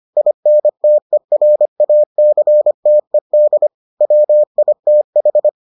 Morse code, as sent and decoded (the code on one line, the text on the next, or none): .. -. - . .-. .- -.-. - . -.. / .-- .. - ....
INTERACTED WITH